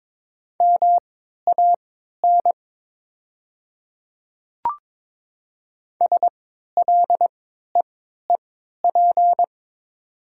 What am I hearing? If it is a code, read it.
Morse code, heard as a keyed tone, about 22 wpm, spaced as for 12 wpm, 700 Hz. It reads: MAN SLEEP
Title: 3x Character Spacing - Top 500 Words - 22wpm